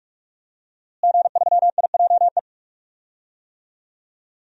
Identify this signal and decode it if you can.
Morse code, heard as a keyed tone, about 46 wpm, 700 Hz. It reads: G3IJE
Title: DXCC-100 Call Signs - 5 Characters - Repeated 2x - 45wpm